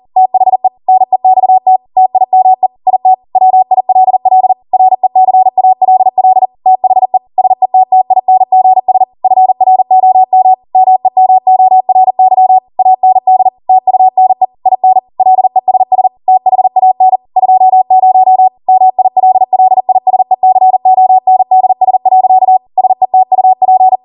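Morse code, heard as a keyed tone, about 40 wpm, 750 Hz. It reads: THE NEXT TIME IT WILL RECALL THE SETTINGS FROM MEMORY AND TUNE IN LESS THAN 20 MILLISECONDS. SETUP
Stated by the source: ARRL 40 WPM Morse code practice transmission.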